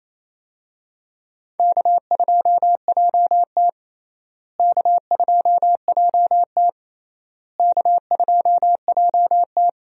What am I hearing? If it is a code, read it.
Morse code, heard as a keyed tone, about 28 wpm, 700 Hz. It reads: K2JT K2JT K2JT
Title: USA Call Signs - 4 Characters - Repeated 3x - 28wpm